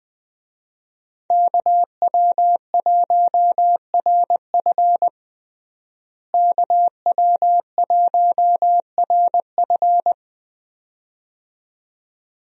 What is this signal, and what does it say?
Morse code, heard as a keyed tone, about 21 wpm, 700 Hz. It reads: KW1RF KW1RF
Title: USA Call Signs - 5 Characters - Repeated 2x - 20wpm